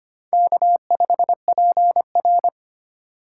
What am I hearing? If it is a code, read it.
Morse code, heard as a keyed tone, about 25 wpm, 700 Hz. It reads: K5PR
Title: USA Call Signs - 2x Character Spacing - 25wpm